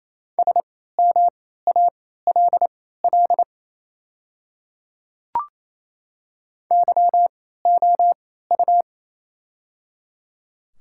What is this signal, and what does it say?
Morse code, heard as a keyed tone, about 27 wpm, spaced as for 16 wpm, 700 Hz. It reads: SMALL YOU
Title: 3x Character Spacing - Top 200 Words - 28wpm